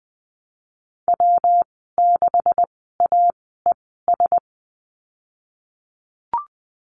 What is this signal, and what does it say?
Morse code, heard as a keyed tone, about 20 wpm, spaced as for 15 wpm, 700 Hz. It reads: W6AES
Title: 128 Lesson 21g - Number 6 - Call Signs 20wpm